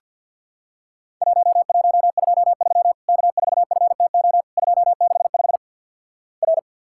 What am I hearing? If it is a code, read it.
Morse code, heard as a keyed tone, about 50 wpm, 700 Hz. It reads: 1123 K4FTO 265 R
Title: Worked All Europe DX Contest - QTC Exchange with Repeats - 50wpm